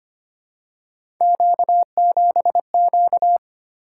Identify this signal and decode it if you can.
Morse code, heard as a keyed tone, about 25 wpm, 700 Hz. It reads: Q7Q
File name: Three Letter-Number Combinations - 25wpm